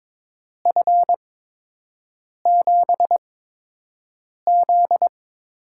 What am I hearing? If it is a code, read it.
Morse code, heard as a keyed tone, about 22 wpm, spaced as for 6 wpm, 700 Hz. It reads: F7Z
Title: USA Call Signs - 6 Characters - 8x Character Spacing - 22wpm